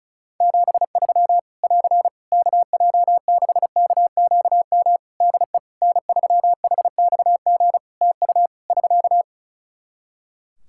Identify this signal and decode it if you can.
Morse code, heard as a keyed tone, about 36 wpm, 700 Hz. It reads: 73 <AR> KJ6KQM DE N3HXG TU <SK>
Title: Practice QSO - Exchange 4 - Ending - 35wpm